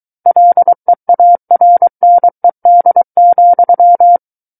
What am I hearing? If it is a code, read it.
Morse code, heard as a keyed tone, about 23 wpm, 700 Hz. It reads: LEARNED,
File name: Book One / 14-Chapter 14 - 23 WPM